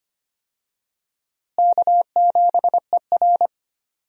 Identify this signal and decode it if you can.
Morse code, heard as a keyed tone, about 25 wpm, 700 Hz. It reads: K7ER
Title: USA Call Signs - 4 Characters - Repeated 2x - 25wpm